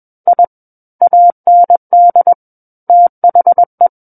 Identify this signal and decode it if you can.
Morse code, heard as a keyed tone, about 21 wpm, 700 Hz. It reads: I AND THE
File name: Book One / 12-Chapter 12 - 21 WPM